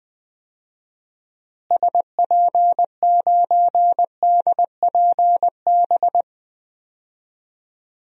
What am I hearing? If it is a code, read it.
Morse code, heard as a keyed tone, about 20 wpm, 700 Hz. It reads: SP9DPB
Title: DXCC-100 Call Signs - 6 Characters - Repeated 3x - 20wpm